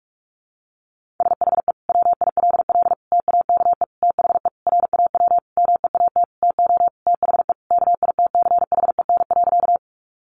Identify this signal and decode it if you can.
Morse code, heard as a keyed tone, about 44 wpm, 700 Hz. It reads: SHE WILL TAKE THE RAW MEAT TO THE KITCHEN.
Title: Sentences from Top 1500 Words 45wpm